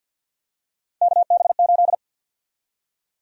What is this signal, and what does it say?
Morse code, heard as a keyed tone, about 51 wpm, 700 Hz. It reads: KB8
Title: USA Call Sign Prefixes - Repeated 2x - 50wpm